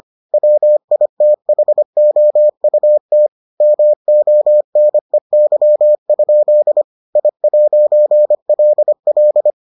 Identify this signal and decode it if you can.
Morse code, heard as a keyed tone, about 25 wpm, 600 Hz. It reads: WITHOUT MONEY? I'LL